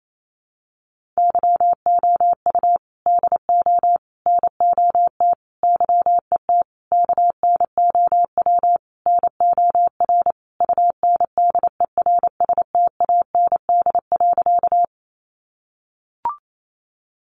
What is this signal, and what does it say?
Morse code, heard as a keyed tone, about 28 wpm, 700 Hz. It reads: YOU DO NOT YET KNOW NOR UNDERSTAND.
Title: Sentences from Top 1500 Words 28wpm